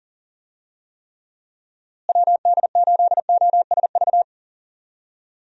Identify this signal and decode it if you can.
Morse code, heard as a keyed tone, about 40 wpm, 700 Hz. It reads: WD8OSV
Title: USA Call Signs - 6 Characters - Repeated 3x - 40wpm